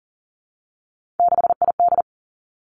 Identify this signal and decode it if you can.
Morse code, heard as a keyed tone, about 40 wpm, 700 Hz. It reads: 6ID
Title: Three Challenging Letter-Number Combinations 40wpm